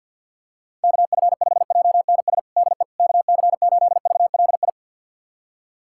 Morse code, heard as a keyed tone, about 52 wpm, 690 Hz. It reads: KF5JNS DE KC8VLI
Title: USA Callsign DE Callsign - 6 Characters - 50wpm